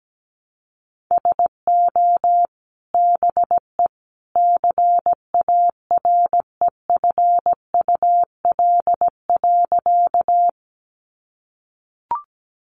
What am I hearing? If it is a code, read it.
Morse code, heard as a keyed tone, about 17 wpm, 700 Hz. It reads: SO BE CAREFUL.
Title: A Sound Of Thunder 17wpm